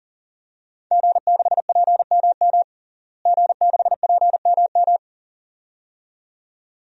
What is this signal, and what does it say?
Morse code, heard as a keyed tone, about 40 wpm, 700 Hz. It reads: G6PMM G6PMM
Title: DXCC-100 Call Signs - 5 Characters - Repeated 2x - 40wpm